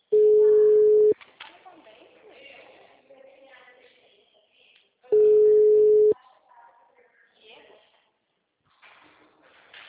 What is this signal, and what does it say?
Ringback tone (Europe, 425 Hz), with speech.